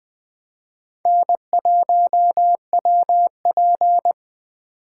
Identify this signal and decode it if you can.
Morse code, heard as a keyed tone, about 21 wpm, 700 Hz. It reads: N1WP
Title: Speed Racing - USA 1x2 Call Signs 17wpm